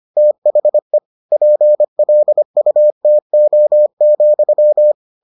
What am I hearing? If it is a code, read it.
Morse code, heard as a keyed tone, about 25 wpm, 600 Hz. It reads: THE PLUTO,